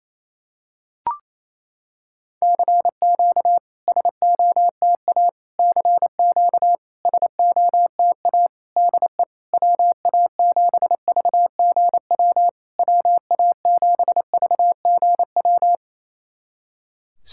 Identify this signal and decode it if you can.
Morse code, heard as a keyed tone, about 28 wpm, 690 Hz. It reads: CQ SOTA CQ SOTA DE WA7VGW WA7VGW
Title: SOTA QSO Practice 28wpm